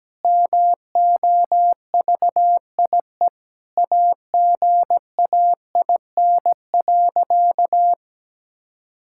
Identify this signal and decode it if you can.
Morse code, heard as a keyed tone, about 17 wpm, 700 Hz. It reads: MOVIE AGAIN.
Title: Sentences from Top 1000 Words - Review Part 5 - 17wpm